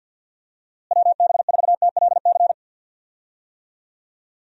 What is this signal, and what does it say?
Morse code, heard as a keyed tone, about 51 wpm, 700 Hz. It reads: WB4TLC
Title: USA Call Signs - 6 Characters - Repeated 2x - 50wpm